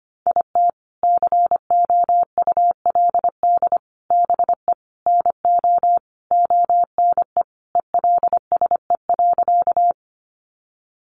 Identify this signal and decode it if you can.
Morse code, heard as a keyed tone, about 25 wpm, 700 Hz. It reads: IT COULD BE NO ONE ELSE.
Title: Sentences from Top 800 Words 25wpm